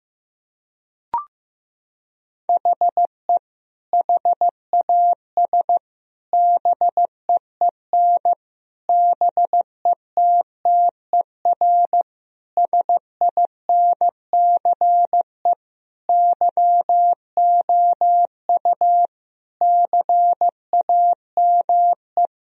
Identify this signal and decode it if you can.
Morse code, heard as a keyed tone, about 15 wpm, 700 Hz. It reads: HE HAS BEEN BETTER SINCE YOU CAME
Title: Sentences from Top 1000 Words - Review Part 1 - 15wpm